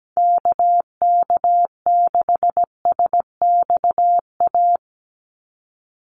Morse code, heard as a keyed tone, about 17 wpm, 700 Hz. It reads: KK6SXA
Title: USA Call Signs 17wpm